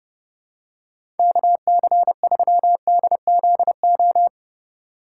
Morse code, heard as a keyed tone, about 30 wpm, 700 Hz. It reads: KC3DZO
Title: USA Call Signs - 3x Character Spacing - 30wpm